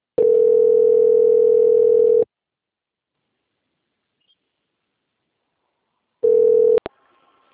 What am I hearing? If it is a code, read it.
Ringback tone (North America), 440 + 480 Hz.